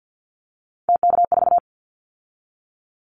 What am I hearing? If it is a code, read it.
Morse code, heard as a keyed tone, about 50 wpm, 700 Hz. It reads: TK4
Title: Callsign Prefixes for Top 100 DXCC - 50wpm